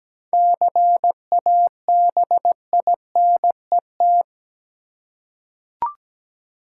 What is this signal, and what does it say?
Morse code, heard as a keyed tone, about 17 wpm, 700 Hz. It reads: CABINET
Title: Seven Letter Words - 17wpm